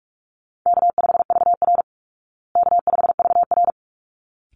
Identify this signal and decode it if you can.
Morse code, heard as a keyed tone, about 44 wpm, 700 Hz. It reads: K5VR K5VR
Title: Call Signs Repeated 3x - 45wpm